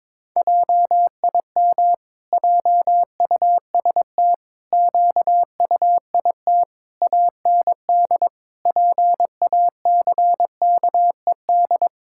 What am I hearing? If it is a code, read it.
Morse code, heard as a keyed tone, about 21 wpm, 700 Hz. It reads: JIM JUST QUIT AND PACKED
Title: Pangram Challenge v2 - 22wpm